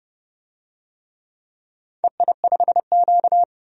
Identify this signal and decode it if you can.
Morse code, heard as a keyed tone, about 29 wpm, 700 Hz. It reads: EI5Q
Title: DXCC-100 Call Signs - 4 Characters - 30wpm